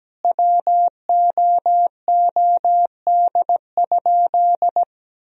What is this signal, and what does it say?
Morse code, heard as a keyed tone, about 17 wpm, 700 Hz. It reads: WOOD?
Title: Sentences from Top 1000 Words - Review Part 2 - 17wpm